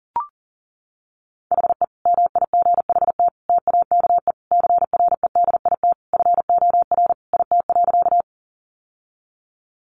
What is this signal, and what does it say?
Morse code, heard as a keyed tone, about 37 wpm, 700 Hz. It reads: HE MIGHT TAKE CREDIT FOR IT.
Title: Sentences from Top 1000 Words 40wpm